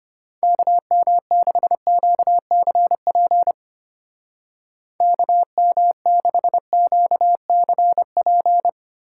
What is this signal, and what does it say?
Morse code, heard as a keyed tone, about 32 wpm, 700 Hz. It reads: KM6QCP KM6QCP